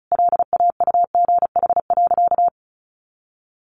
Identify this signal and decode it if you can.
Morse code, heard as a keyed tone, about 34 wpm, 700 Hz. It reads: LAUGH.
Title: Sentences from Top 1500 Words 35wpm